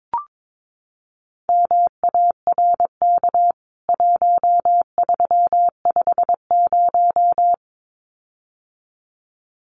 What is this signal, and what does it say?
Morse code, heard as a keyed tone, about 22 wpm, 700 Hz. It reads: MARK 1350
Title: CWT Exchange 22wpm